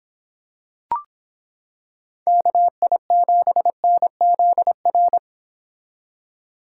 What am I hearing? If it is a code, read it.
Morse code, heard as a keyed tone, about 26 wpm, 700 Hz. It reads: KI7NZR